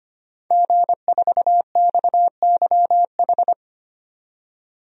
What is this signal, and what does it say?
Morse code, heard as a keyed tone, about 25 wpm, 700 Hz. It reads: G4XYH
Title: DXCC-100 Call Signs - 5 Characters - 25wpm